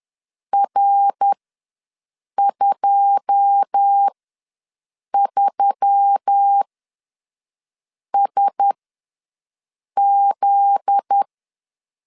Morse code, heard as a keyed tone, about 10 wpm, spaced as for 5 wpm, 800 Hz. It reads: R23SZ
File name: LEZIONI / 9 2 ws 10wpm 4130